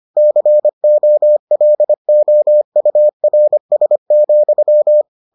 Morse code, heard as a keyed tone, about 25 wpm, 600 Hz. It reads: COLOURS,